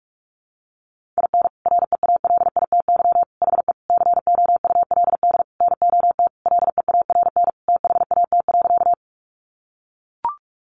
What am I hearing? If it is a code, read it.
Morse code, heard as a keyed tone, about 44 wpm, 700 Hz. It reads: IN REALITY HE COULD NOT LEARN THAT.
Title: Sentences from Top 1200 Words 45wpm